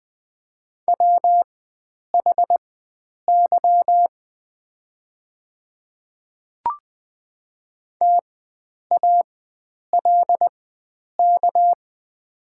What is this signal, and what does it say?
Morse code, heard as a keyed tone, about 20 wpm, spaced as for 9 wpm, 700 Hz. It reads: WHY TALK
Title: 4x Character Spacing - Top 500 Words - 20wpm